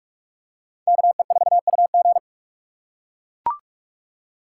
Morse code, heard as a keyed tone, about 39 wpm, 700 Hz. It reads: KE4UG